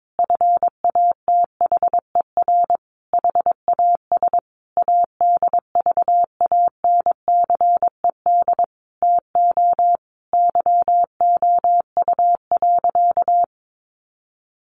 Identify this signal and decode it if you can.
Morse code, heard as a keyed tone, about 22 wpm, 700 Hz. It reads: FATHER HAS ADVANCED TO YOU.
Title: Sentences from Top 1200 Words 22wpm